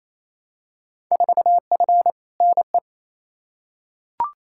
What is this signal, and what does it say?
Morse code, heard as a keyed tone, about 28 wpm, 700 Hz. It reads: 4F NE
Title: Field Day Exchanges Encore - 28wpm